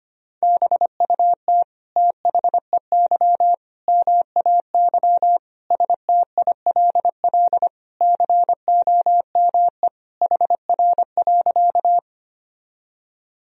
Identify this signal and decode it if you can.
Morse code, heard as a keyed tone, about 25 wpm, 700 Hz. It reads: BUT THEY MAY STILL COME HR.
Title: Sentences from Top 300 Words and Abbreviations 25wpm